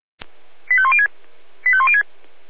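Five-tone selective call (ZVEI), heard as quick runs of five tones, tones of about 70 ms. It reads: 85106 85106
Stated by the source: ZVEI five-tone selective call received by SDR.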